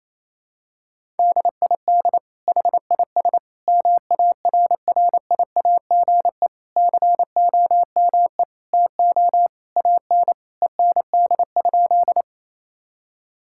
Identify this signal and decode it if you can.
Morse code, heard as a keyed tone, about 28 wpm, 700 Hz. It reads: DID HIS MARRIAGE COME TO AN END?
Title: Sentences from Top 1000 Words - Review Part 5 - 28wpm